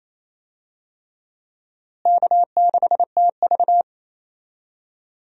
Morse code, heard as a keyed tone, about 28 wpm, 700 Hz. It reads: K6TV
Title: USA Call Signs - 4 Characters - Repeated 3x - 28wpm